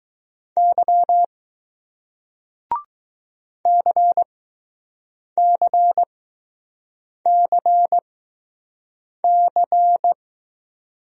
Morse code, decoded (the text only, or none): Y CCCC